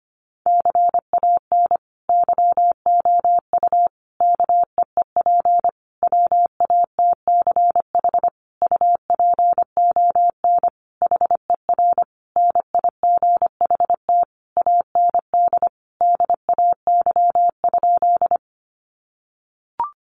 Morse code, heard as a keyed tone, about 25 wpm, 700 Hz. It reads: CAN YOU KEEP WATCH UPON HER NIGHT AND DAY?
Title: Sentences from Top 1100 Words 25wpm